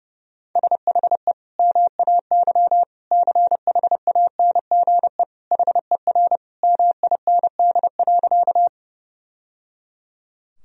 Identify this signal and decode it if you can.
Morse code, heard as a keyed tone, about 32 wpm, 700 Hz. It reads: SHE MAY CHANGE HER MIND.
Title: Sentences from Top 1000 Words - Review Part 1 - 30wpm